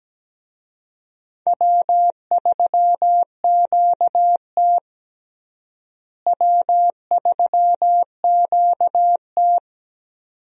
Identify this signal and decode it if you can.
Morse code, heard as a keyed tone, about 17 wpm, 700 Hz. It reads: W3QT W3QT
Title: USA Call Signs - 4 Characters - Repeated 2x - 17wpm